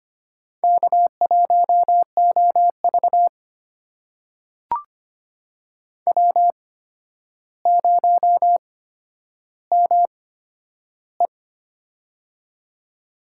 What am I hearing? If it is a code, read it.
Morse code, heard as a keyed tone, about 25 wpm, spaced as for 11 wpm, 700 Hz. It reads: K1OV W0ME
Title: USA Call Signs - 4 Characters - 8x Character Spacing - 25wpm